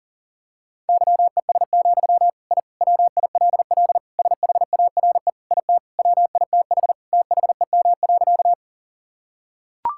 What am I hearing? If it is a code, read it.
Morse code, heard as a keyed tone, about 40 wpm, 700 Hz. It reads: YES, I WILL SHARE IT WITH THEM.
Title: Sentences from Top 1000 Words - Review Part 6 - 40wpm